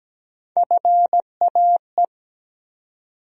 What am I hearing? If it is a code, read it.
Morse code, heard as a keyed tone, about 17 wpm, 700 Hz. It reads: FAE
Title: USA Call Sign Postfixes - 3 Letters 17wpm